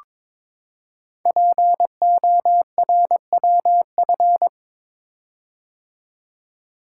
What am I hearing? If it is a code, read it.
Morse code, heard as a keyed tone, about 22 wpm, 700 Hz. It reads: PORWF
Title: Code Groups - 5 Letters - 22wpm